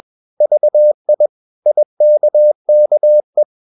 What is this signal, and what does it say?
Morse code, heard as a keyed tone, about 21 wpm, 600 Hz. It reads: VI IKKE
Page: Ferie 20 wpm